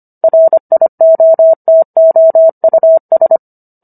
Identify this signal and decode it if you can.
Morse code, heard as a keyed tone, about 25 wpm, 650 Hz. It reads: RIOTOUS